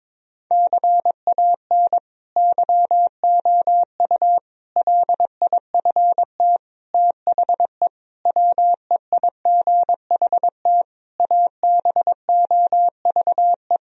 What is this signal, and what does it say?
Morse code, heard as a keyed tone, about 22 wpm, 700 Hz. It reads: CAN YOU LIFT THE WEIGHT ABOVE
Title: Sentences from Top 1000 Words - Review Part 3 - 22wpm